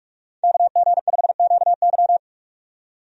Morse code, heard as a keyed tone, about 46 wpm, 700 Hz. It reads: KG5QY
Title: USA Call Signs - 5 Characters - 45wpm